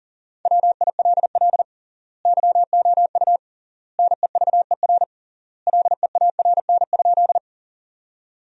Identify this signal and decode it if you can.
Morse code, heard as a keyed tone, about 40 wpm, 700 Hz. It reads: WILL YOU NEVER LEARN?
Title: 3x Word-Spacing - Sentences from Top 300 Words - 40wpm